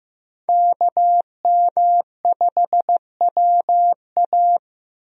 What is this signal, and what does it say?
Morse code, heard as a keyed tone, about 15 wpm, 700 Hz. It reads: KM5WA
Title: USA Call Signs - 5 Characters - 15wpm